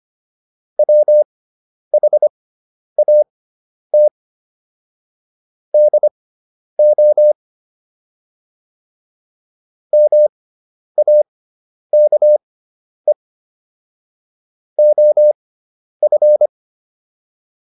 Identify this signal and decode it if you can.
Morse code, heard as a keyed tone, about 25 wpm, spaced as for 10 wpm, 600 Hz. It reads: WHAT DO MAKE OF